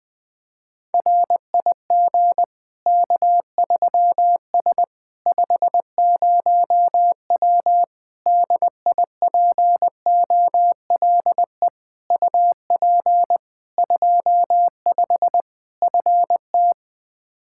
Morse code, heard as a keyed tone, about 20 wpm, 700 Hz. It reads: RIG K3S 50W DIPOLE UP 25 FT
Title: Practice QSO - Exchange 2 - Info - 20wpm